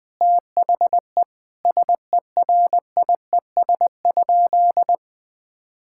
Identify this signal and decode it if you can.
Morse code, heard as a keyed tone, about 20 wpm, 700 Hz. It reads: THE SERIES?
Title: Sentences from Top 1000 Words - Review Part 5 - 20wpm